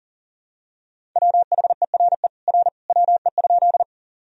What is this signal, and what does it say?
Morse code, heard as a keyed tone, about 40 wpm, 700 Hz. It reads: WHERE R WE?
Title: Sentences from Top 200 Words and Abbreviations 40wpm